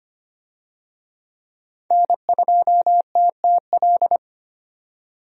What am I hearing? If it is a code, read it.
Morse code, heard as a keyed tone, about 25 wpm, 700 Hz. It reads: N2TTL
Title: USA Call Signs - 5 Characters - 25wpm